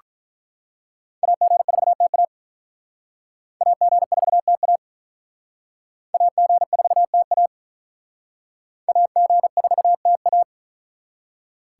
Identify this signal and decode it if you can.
Morse code, heard as a keyed tone, about 42 wpm, 700 Hz. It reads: AG4TA AG4TA AG4TA AG4TA
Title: Speed Racing - USA 2x2 Call Signs 35wpm